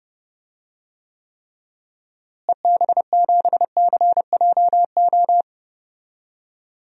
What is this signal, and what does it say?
Morse code, heard as a keyed tone, about 30 wpm, 700 Hz. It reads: EB7CJO
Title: DXCC-100 Call Signs - 6 Characters - Repeated 3x - 30wpm